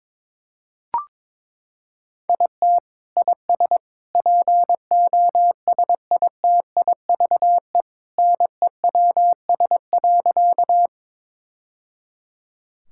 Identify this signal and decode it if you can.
Morse code, heard as a keyed tone, about 22 wpm, 700 Hz. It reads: IT IS POSITIVE NEWS.
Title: Sentences from Top 1000 Words - Review Part 5 - 22wpm